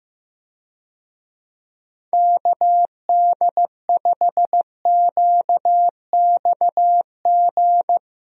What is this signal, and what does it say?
Morse code, heard as a keyed tone, about 15 wpm, 700 Hz. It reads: KD5QXG
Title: USA Call Signs - 6 Characters - Repeated 3x - 15wpm